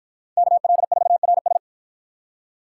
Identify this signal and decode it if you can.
Morse code, heard as a keyed tone, about 51 wpm, 700 Hz. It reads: KL4RS